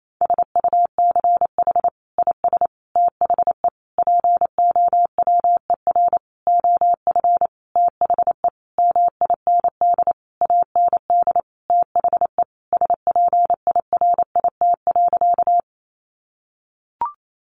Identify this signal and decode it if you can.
Morse code, heard as a keyed tone, about 28 wpm, 700 Hz. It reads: SUCH IS THE POWER OF THE MIND AND THE SPIRIT.
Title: Sentences from Top 1400 Words 28wpm